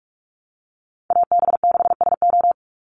Morse code, heard as a keyed tone, about 44 wpm, 700 Hz. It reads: AB6SO
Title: NA SOTA Chaser Callsigns 45wpm